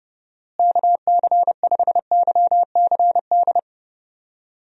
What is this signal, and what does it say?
Morse code, heard as a keyed tone, about 30 wpm, 700 Hz. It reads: KC5YCD
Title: USA Call Signs - 6 Characters - Repeated 2x - 30wpm